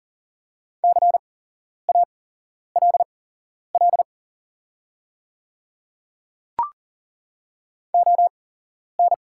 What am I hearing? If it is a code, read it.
Morse code, heard as a keyed tone, about 40 wpm, spaced as for 11 wpm, 690 Hz. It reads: CALL ON
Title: 8x Character Spacing - Top 100 Words - 40wpm